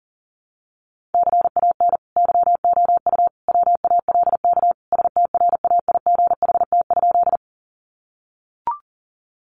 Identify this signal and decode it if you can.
Morse code, heard as a keyed tone, about 40 wpm, 700 Hz. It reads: CAN YOU WALK STRAIGHT?
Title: Sentences from Top 1000 Words 40wpm